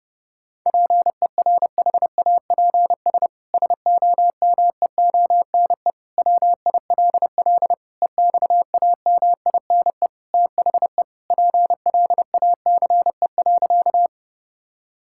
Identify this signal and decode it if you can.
Morse code, heard as a keyed tone, about 30 wpm, 700 Hz. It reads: PERHAPS SOMEONE WILL EXAMINE THE PLACE.
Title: Sentences from Top 1600 Words 30wpm